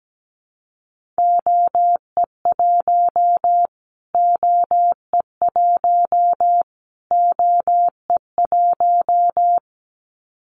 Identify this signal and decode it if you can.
Morse code, heard as a keyed tone, about 17 wpm, 700 Hz. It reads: OE1 OE1 OE1
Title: Callsign Prefixes for Top 25 DXCC - Repeated 3x - 17wpm